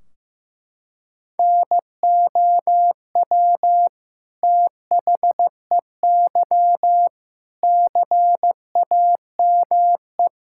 Morse code, heard as a keyed tone, about 15 wpm, 700 Hz. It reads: NOW THEY CAME
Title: Sentences from Top 1600 Words 15wpm